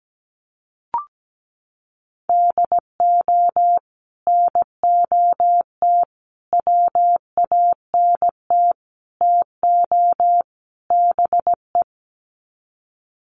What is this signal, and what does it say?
Morse code, heard as a keyed tone, about 17 wpm, 700 Hz. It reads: DO NOT WANT TO BE
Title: Sets of 5 Words 17wpm